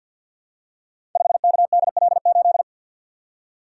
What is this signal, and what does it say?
Morse code, heard as a keyed tone, about 51 wpm, 700 Hz. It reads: 5KDL8
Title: Code Groups - 5 Letter:Numbers - 50wpm